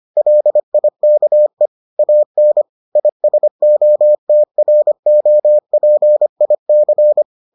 Morse code, heard as a keyed tone, about 25 wpm, 600 Hz. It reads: LIKE AN ISOTROPIC